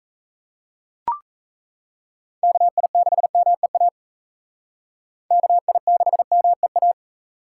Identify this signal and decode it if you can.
Morse code, heard as a keyed tone, about 36 wpm, 700 Hz. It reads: KI6MEA KI6MEA